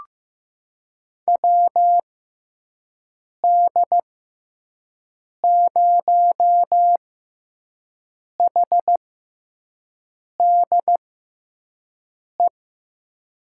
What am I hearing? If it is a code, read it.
Morse code, heard as a keyed tone, about 15 wpm, spaced as for 5 wpm, 700 Hz. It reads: WD0HDE
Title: USA Call Signs - 6 Characters - 6x Character Spacing - 15wpm